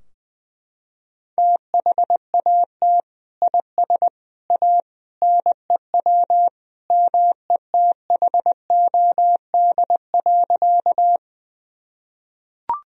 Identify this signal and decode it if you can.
Morse code, heard as a keyed tone, about 20 wpm, 700 Hz. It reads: THAT IS A NEW METHOD.
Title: Sentences from Top 1000 Words - Review Part 3 - 20wpm